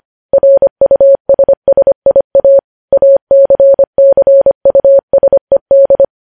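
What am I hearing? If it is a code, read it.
Morse code, heard as a keyed tone, about 25 wpm, 550 Hz. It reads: RUSSIA ACCUSED
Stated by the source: Morse code practice of news headlines sent at 25 wpm.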